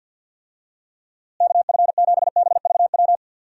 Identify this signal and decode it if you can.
Morse code, heard as a keyed tone, about 46 wpm, 700 Hz. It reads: KF7BVW
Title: USA Call Signs - 6 Characters - Repeated 3x - 50wpm